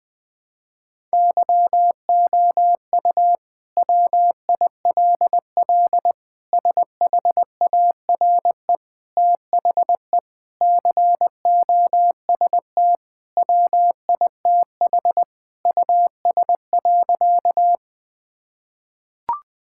Morse code, heard as a keyed tone, about 21 wpm, 700 Hz. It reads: YOU WILL SHARE THE COST WITH US.
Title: Sentences from Top 1000 Words - Review Part 6 - 20wpm